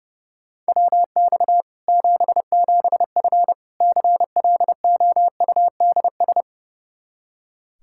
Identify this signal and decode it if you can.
Morse code, heard as a keyed tone, about 30 wpm, 700 Hz. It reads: WX 77F CLOUDS
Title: Practice QSO - Exchange 2 - Info - 30wpm